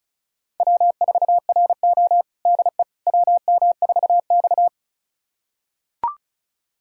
Morse code, heard as a keyed tone, about 34 wpm, 700 Hz. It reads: W4RO DE WM4X
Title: USA Callsign DE Callsign - 4 Characters - 35wpm